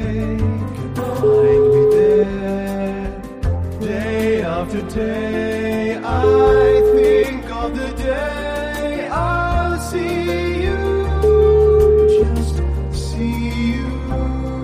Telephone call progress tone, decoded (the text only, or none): ringback tone